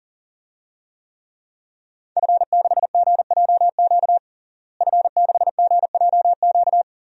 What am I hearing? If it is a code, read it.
Morse code, heard as a keyed tone, about 37 wpm, 690 Hz. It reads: F6GJQ F6GJQ
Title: DXCC-100 Call Signs - 5 Characters - Repeated 2x - 40wpm